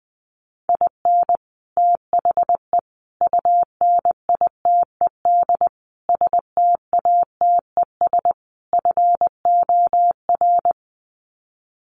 Morse code, heard as a keyed tone, about 20 wpm, 700 Hz. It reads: IN THE UNITED STATES FOR
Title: Sets of 5 Words Encore2 20wpm